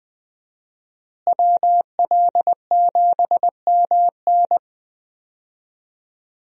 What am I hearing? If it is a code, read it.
Morse code, heard as a keyed tone, about 21 wpm, 700 Hz. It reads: WL7MN
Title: USA Call Signs - 5 Characters - Repeated 3x - 20wpm